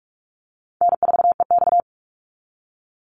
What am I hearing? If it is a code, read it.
Morse code, heard as a keyed tone, about 46 wpm, 690 Hz. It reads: N4EX
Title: NA SOTA Chaser Callsigns 45wpm